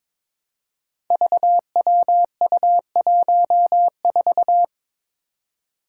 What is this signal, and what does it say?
Morse code, heard as a keyed tone, about 22 wpm, 700 Hz. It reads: VWU14
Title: Five Challenging Letter-Number Combinations 22wpm